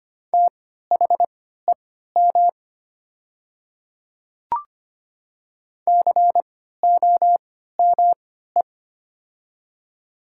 Morse code, heard as a keyed tone, about 25 wpm, spaced as for 14 wpm, 690 Hz. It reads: THEM COME
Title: 3x Character Spacing - Top 200 Words - 25wpm